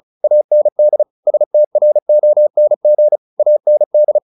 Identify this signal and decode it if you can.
Morse code, heard as a keyed tone, about 36 wpm, 600 Hz. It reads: AND STRONG AND
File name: the old man and the sea 2 35wpm